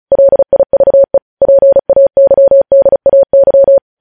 Morse code, heard as a keyed tone, about 36 wpm, 550 Hz. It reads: LIVE PAYDAY
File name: podcast / News headlines-1x-35x35